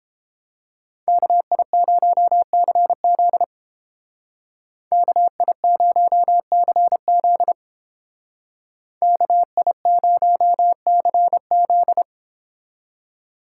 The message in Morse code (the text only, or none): KI0CZ KI0CZ KI0CZ